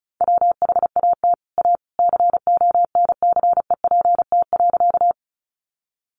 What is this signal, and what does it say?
Morse code, heard as a keyed tone, about 34 wpm, 700 Hz. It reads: WHAT A CONCEPT.
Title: Sentences from Top 1100 Words 35wpm